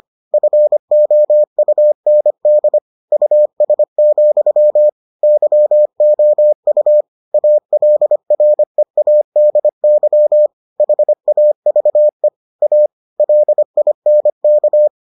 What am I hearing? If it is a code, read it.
Morse code, heard as a keyed tone, about 25 wpm, 600 Hz. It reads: FOUND US, YOU ALREADY HAVE A LINK